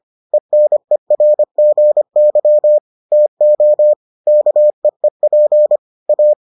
Morse code, heard as a keyed tone, about 25 wpm, 600 Hz. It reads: ENERGY TO KEEP A